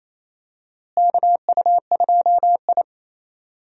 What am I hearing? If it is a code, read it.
Morse code, heard as a keyed tone, about 28 wpm, 700 Hz. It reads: KU2I
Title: USA Call Signs - 4 Characters - Repeated 2x - 28wpm